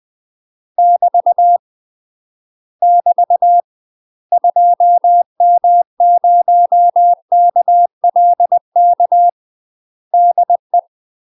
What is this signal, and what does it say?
Morse code, heard as a keyed tone, about 20 wpm, 700 Hz. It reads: <BT> <BT> 2M0KLK DE
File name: UK MASTER QSOs TRAINING / 20WPM / UK MASTER QSOs 0052